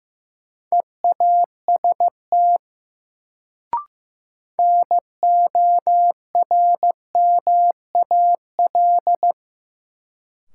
Morse code, heard as a keyed tone, about 15 wpm, 700 Hz. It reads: EAST NORMAL
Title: Sentences from Top 1200 Words - New Word Drill Alt - 15wpm